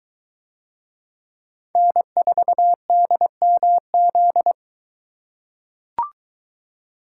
Morse code, heard as a keyed tone, about 22 wpm, 700 Hz. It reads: N4DMZ